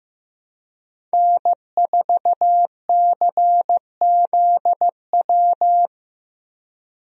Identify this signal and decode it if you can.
Morse code, heard as a keyed tone, about 15 wpm, 700 Hz. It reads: N4CZW
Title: USA Call Signs - 5 Characters - Repeated 2x - 15wpm